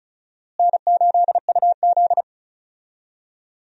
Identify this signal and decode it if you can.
Morse code, heard as a keyed tone, about 34 wpm, 700 Hz. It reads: N8UZ
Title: USA Call Signs - 4 Characters - Repeated 3x - 35wpm